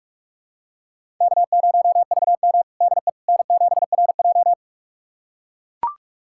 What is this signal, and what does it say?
Morse code, heard as a keyed tone, about 41 wpm, 700 Hz. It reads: K0VM DE N7RJ
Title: USA Callsign DE Callsign - 4 Characters - 45wpm